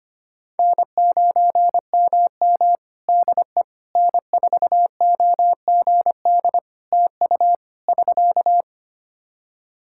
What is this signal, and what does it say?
Morse code, heard as a keyed tone, about 25 wpm, 700 Hz. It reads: N9MM DE N4OGD TU <SK>
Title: K1USN SST QSO Practice 25wpm